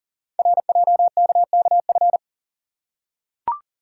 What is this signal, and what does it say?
Morse code, heard as a keyed tone, about 39 wpm, 700 Hz. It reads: RJKKF
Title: Code Groups - 5 Letters - 40wpm